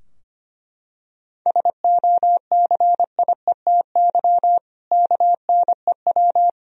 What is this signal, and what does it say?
Morse code, heard as a keyed tone, about 25 wpm, 700 Hz. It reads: SOCIETY KNEW
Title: Sentences from Top 1000 Words - Review Part 4 - 25wpm